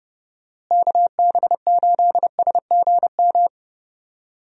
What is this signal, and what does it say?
Morse code, heard as a keyed tone, about 30 wpm, 700 Hz. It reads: KB8SGM
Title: USA Call Signs - 6 Characters - Repeated 3x - 30wpm